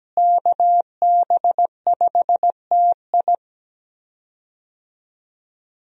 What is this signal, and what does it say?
Morse code, heard as a keyed tone, about 17 wpm, 700 Hz. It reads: KB5TI
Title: USA Call Signs - 5 Characters - Repeated 2x - 17wpm